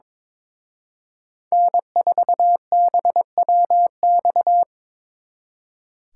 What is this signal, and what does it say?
Morse code, heard as a keyed tone, about 26 wpm, 700 Hz. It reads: N4BWX
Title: Speed Racing - USA Call Signs 22wpm